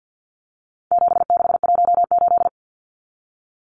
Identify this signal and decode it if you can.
Morse code, heard as a keyed tone, about 50 wpm, 700 Hz. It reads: 7618
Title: Numbers - 4-Digit 50wpm